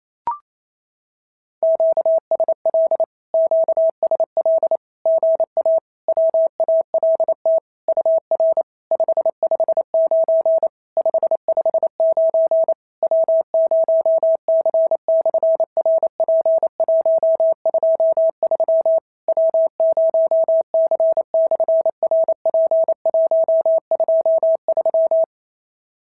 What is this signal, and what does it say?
Morse code, heard as a keyed tone, about 28 wpm, 650 Hz. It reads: QSL QSL GA WALT UR 559 559 W0C/RP123 W0C/RP123
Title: SOTA QSO Practice 28wpm